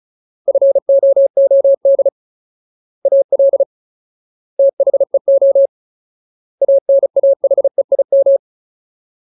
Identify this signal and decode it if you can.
Morse code, heard as a keyed tone, about 34 wpm, 550 Hz. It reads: FOOD AL THEO ANAHEIM
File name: Ultimate Word List / 35WPM Ultimate